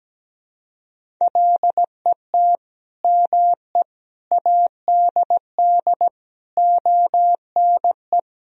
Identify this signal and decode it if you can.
Morse code, heard as a keyed tone, about 17 wpm, 700 Hz. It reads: LET ME ADD ONE
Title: Sentences from Top 1000 Words - Review Part 1 - 17wpm